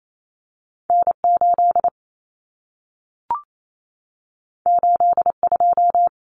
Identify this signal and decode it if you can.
Morse code, heard as a keyed tone, about 28 wpm, 700 Hz. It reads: N8 82
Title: Common Cut Numbers - 2-Digit 28wpm